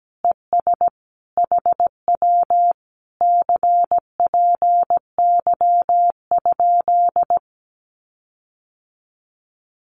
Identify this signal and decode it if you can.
Morse code, heard as a keyed tone, about 17 wpm, 700 Hz. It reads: ES HW CPY?
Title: Common QSO Phrases Encore 17wpm